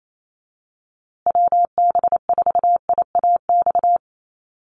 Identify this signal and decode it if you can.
Morse code, heard as a keyed tone, about 28 wpm, 700 Hz. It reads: WB4IAX
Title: USA Call Signs Repeated 3x 28wpm